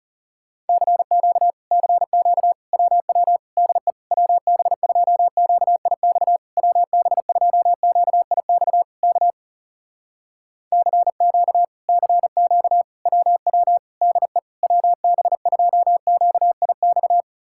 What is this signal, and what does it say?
Morse code, heard as a keyed tone, about 40 wpm, 700 Hz. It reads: CQ CQ WW DE WB2QIX WB2QIX K CQ CQ WW DE WB2QIX